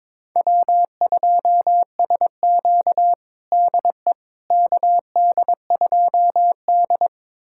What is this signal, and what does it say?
Morse code, heard as a keyed tone, about 21 wpm, 700 Hz. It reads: W2SQ DE KD2D
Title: USA Callsign DE Callsign - 4 Characters - 22wpm